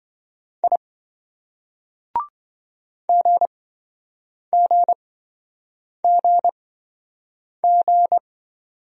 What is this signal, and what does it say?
Morse code, heard as a keyed tone, about 28 wpm, spaced as for 7 wpm, 700 Hz. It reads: I GGGG